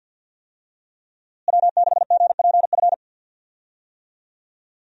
Morse code, heard as a keyed tone, about 50 wpm, 700 Hz. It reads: W6GPF
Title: USA Call Signs - 5 Characters - Repeated 3x - 50wpm